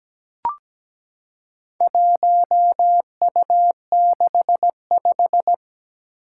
Morse code, heard as a keyed tone, about 17 wpm, 700 Hz. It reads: 1U65
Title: Four Challenging Letter-Number Combinations 17wpm